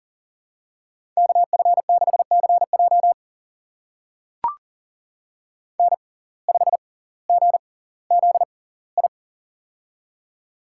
Morse code, heard as a keyed tone, about 38 wpm, spaced as for 20 wpm, 700 Hz. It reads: KF6CJ N5GZI
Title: USA Call Signs - 5 Characters - 6x Character Spacing - 40wpm